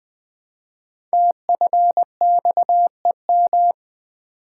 Morse code, heard as a keyed tone, about 20 wpm, 700 Hz. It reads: TFXEM
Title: Code Groups - 5 Letters - 20wpm